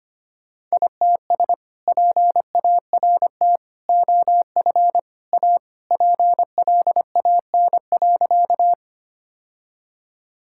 Morse code, heard as a keyed tone, about 25 wpm, 700 Hz. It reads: ITS PART OF A PLAN.
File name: practice / 25wpm / Sentences with 3-Word Phrases - 25wpm